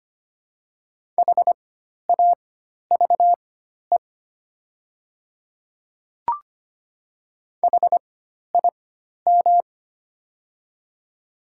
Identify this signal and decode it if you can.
Morse code, heard as a keyed tone, about 25 wpm, spaced as for 12 wpm, 700 Hz. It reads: HAVE HIM
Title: 4x Character Spacing - Top 100 Words - 25wpm